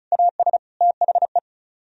Morse code, heard as a keyed tone, about 45 wpm, 700 Hz. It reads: AS THE